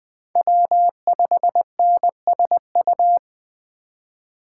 Morse code, heard as a keyed tone, about 24 wpm, 700 Hz. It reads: W5NSU
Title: Speed Racing - USA 1x3 Call Signs 20wpm